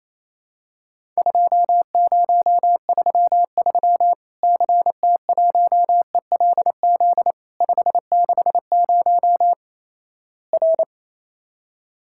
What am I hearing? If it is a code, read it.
Morse code, heard as a keyed tone, about 28 wpm, 700 Hz. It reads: 2033 CT1ELZ 560 R
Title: Worked All Europe DX Contest - QTC Exchange - 28wpm